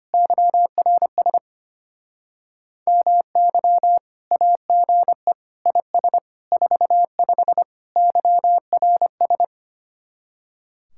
Morse code, heard as a keyed tone, about 31 wpm, 700 Hz. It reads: YRS MY AGE IS 45 YRS
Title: Speed-Racing Common QSO Phrases - 25wpm